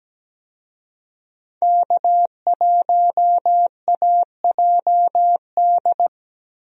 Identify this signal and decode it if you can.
Morse code, heard as a keyed tone, about 17 wpm, 700 Hz. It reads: K1AJD
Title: USA Call Signs - 5 Characters - Repeated 2x - 17wpm